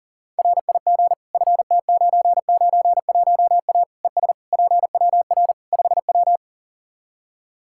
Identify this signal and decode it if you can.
Morse code, heard as a keyed tone, about 40 wpm, 700 Hz. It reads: RIG FT991A ES PWR 5W
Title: Practice QSO - Exchange 2 - Info - 40wpm